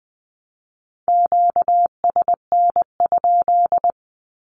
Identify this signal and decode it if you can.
Morse code, heard as a keyed tone, about 20 wpm, 700 Hz. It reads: QSN?
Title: Q-Codes Repeated 3x 20wpm